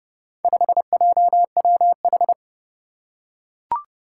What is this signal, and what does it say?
Morse code, heard as a keyed tone, about 32 wpm, 700 Hz. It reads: 5JWH
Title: Four Challenging Letter-Number Combinations 30wpm